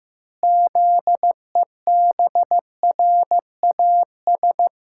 Morse code, heard as a keyed tone, about 16 wpm, 700 Hz. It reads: ZEBRAS
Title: Pangram Challenge v2 - 15wpm